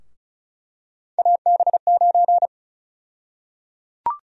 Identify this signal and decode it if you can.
Morse code, heard as a keyed tone, about 34 wpm, 700 Hz. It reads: AB9
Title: USA Call Sign Prefixes - 35wpm